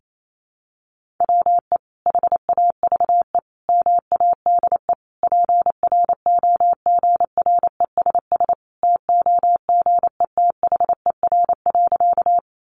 Morse code, heard as a keyed tone, about 28 wpm, 700 Hz. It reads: WE HAVE MADE PROGRESS TOGETHER.
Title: Sentences from Top 1400 Words 28wpm